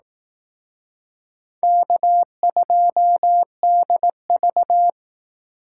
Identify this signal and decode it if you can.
Morse code, heard as a keyed tone, about 18 wpm, 700 Hz. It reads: K2DV
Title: Speed Racing - USA 1x2 Call Signs 15wpm